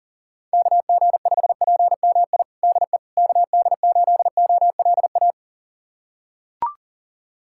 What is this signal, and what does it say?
Morse code, heard as a keyed tone, about 40 wpm, 700 Hz. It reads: KG5PMI DE KD8OLA
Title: USA Callsign DE Callsign - 6 Characters - 40wpm